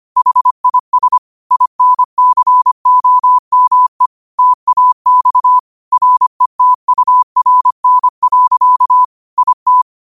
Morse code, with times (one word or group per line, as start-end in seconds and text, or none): 0.2-1.2 HIS
1.5-4.1 INCOME
4.4-5.6 TAX
5.9-9.0 RETURN.
9.4-9.8 IT